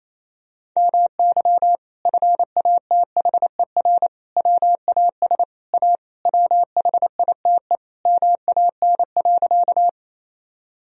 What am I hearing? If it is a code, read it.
Morse code, heard as a keyed tone, about 28 wpm, 700 Hz. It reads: MY FATHER WAS A WHITE MAN.
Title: Sentences from Top 1000 Words - Review Part 1 - 28wpm